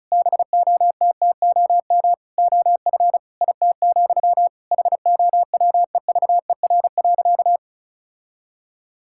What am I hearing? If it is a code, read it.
Morse code, heard as a keyed tone, about 34 wpm, 700 Hz. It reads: BOTTOM OF IT, HOWEVER.
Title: Sentences from Top 1000 Words - Review Part 6 - 35wpm